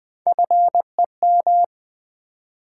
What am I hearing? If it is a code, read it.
Morse code, heard as a keyed tone, about 20 wpm, 700 Hz. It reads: FEM
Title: Build A Word - 3 Letters At A Time - 6-Letter Words - 20wpm